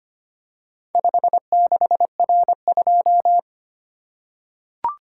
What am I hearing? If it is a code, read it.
Morse code, heard as a keyed tone, about 25 wpm, 700 Hz. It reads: 56R2
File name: practice / 25wpm / Four Letter-Number Combinations - 25wpm